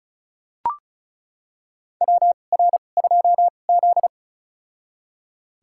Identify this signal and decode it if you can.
Morse code, heard as a keyed tone, about 34 wpm, spaced as for 25 wpm, 700 Hz. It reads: WR2Z
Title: USA Call Signs - 2x Character Spacing - 35wpm